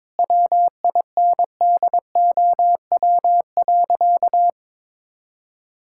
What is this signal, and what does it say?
Morse code, heard as a keyed tone, about 22 wpm, 700 Hz. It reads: WINDOW.
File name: practice / 22wpm / Sentences with 2-Word Phrases - 22wpm